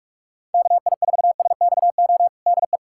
Morse code, heard as a keyed tone, about 46 wpm, 690 Hz. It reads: KI4SXO DE
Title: USA Callsign DE Callsign 45wpm